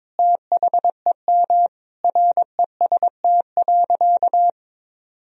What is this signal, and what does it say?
Morse code, heard as a keyed tone, about 22 wpm, 700 Hz. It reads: THEM REST.
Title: Sentences from Top 1000 Words - Review Part 4 - 22wpm